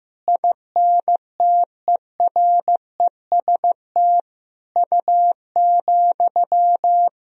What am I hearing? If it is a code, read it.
Morse code, heard as a keyed tone, about 16 wpm, 700 Hz. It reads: INTEREST U,
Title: Sentences from Top 400 Words and Abbreviations 15wpm